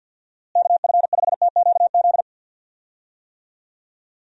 Morse code, heard as a keyed tone, about 46 wpm, 700 Hz. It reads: KF5TQZ
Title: USA Call Signs - 6 Characters - Repeated 2x - 50wpm